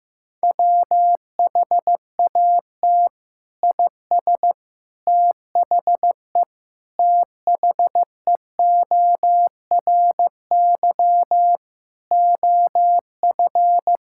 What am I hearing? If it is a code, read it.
Morse code, heard as a keyed tone, about 15 wpm, 700 Hz. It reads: WHAT IS THE THEORY OF
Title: Sentences from Top 1000 Words - Review Part 6 - 15wpm